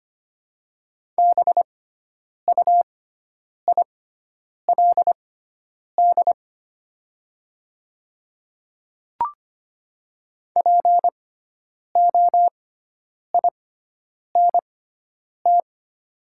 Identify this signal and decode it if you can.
Morse code, heard as a keyed tone, about 25 wpm, spaced as for 9 wpm, 700 Hz. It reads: BUILD POINT
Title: 6x Character Spacing - Top 200 Words - 25wpm